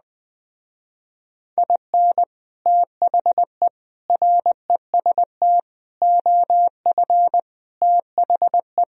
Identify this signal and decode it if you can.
Morse code, heard as a keyed tone, about 20 wpm, 700 Hz. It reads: IN THE REST OF THE
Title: Speed-Racing - Sets of 5 Words 17wpm